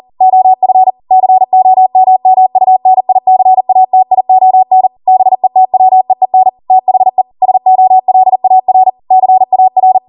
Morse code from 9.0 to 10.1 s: CAR